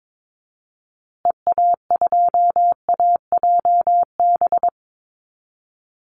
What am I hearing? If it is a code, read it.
Morse code, heard as a keyed tone, about 22 wpm, 700 Hz. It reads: EA2AJB
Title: Call Signs Repeated 2x - 22wpm